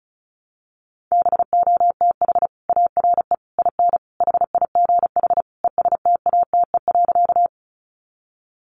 Morse code, heard as a keyed tone, about 34 wpm, 700 Hz. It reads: BOTH ARE IN HIGH ESTATE.
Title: Sentences from Top 1200 Words 35wpm